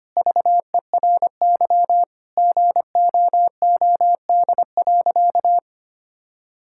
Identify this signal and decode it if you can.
Morse code, heard as a keyed tone, about 25 wpm, 700 Hz. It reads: VERY GOOD.
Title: Sentences from Top 1000 Words - Review Part 5 - 25wpm